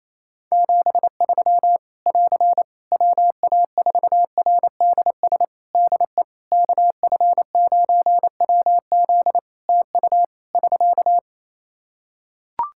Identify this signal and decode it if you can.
Morse code, heard as a keyed tone, about 28 wpm, 700 Hz. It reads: 73 <AR> WA4RDS DE KF9WZ TU <SK>
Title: K1USN SST QSO Practice 28wpm